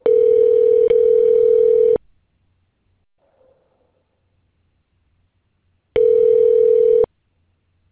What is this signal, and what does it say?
Ringback tone (North America), 440 + 480 Hz.